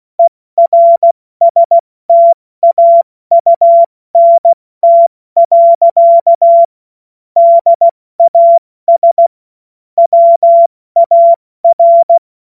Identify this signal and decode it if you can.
Morse code, heard as a keyed tone, about 16 wpm, 680 Hz. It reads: ERSTAUNT. DAS WAR
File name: Morsekurs / Witze / Witz 20. Juni 2023 70 F80